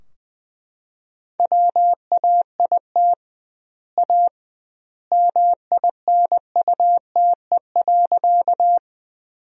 Morse code, heard as a keyed tone, about 20 wpm, 700 Hz. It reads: WAIT A MINUTE.
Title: 2x Word-Spacing - Sentences from Top 500 Words - 20wpm